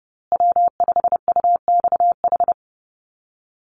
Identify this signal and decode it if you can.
Morse code, heard as a keyed tone, about 30 wpm, 700 Hz. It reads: W5UXH
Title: Call Signs Repeated 2x - 30wpm